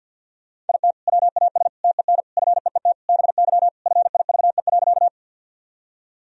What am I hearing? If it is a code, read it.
Morse code, heard as a keyed tone, about 46 wpm, 700 Hz. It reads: IT WAS TEN FEET BY FIVE.
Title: Sentences from Top 1000 Words - Review Part 2 - 50wpm